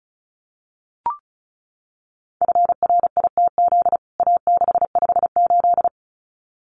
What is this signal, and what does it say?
Morse code, heard as a keyed tone, about 34 wpm, 700 Hz. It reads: FRITZ A658
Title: CWT Exchange 35wpm